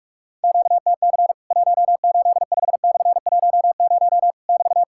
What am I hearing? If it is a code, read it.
Morse code, heard as a keyed tone, about 46 wpm, 700 Hz. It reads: QTC 185/10 <BT>
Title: Worked All Europe DX Contest - QTC Exchange - 45wpm